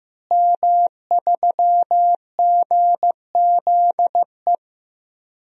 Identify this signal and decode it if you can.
Morse code, heard as a keyed tone, about 15 wpm, 700 Hz. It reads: M3GZE
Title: DXCC-100 Call Signs - 5 Characters - Repeated 2x - 15wpm